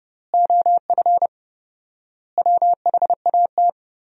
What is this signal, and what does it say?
Morse code, heard as a keyed tone, about 30 wpm, 700 Hz. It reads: OF WHAT
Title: 4x Word-Spacing - Sentences from Top 400 Words - 30wpm